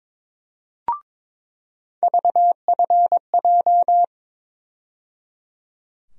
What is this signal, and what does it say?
Morse code, heard as a keyed tone, about 22 wpm, 700 Hz. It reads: VFJ
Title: USA Call Sign Postfixes - 3 Letters 22wpm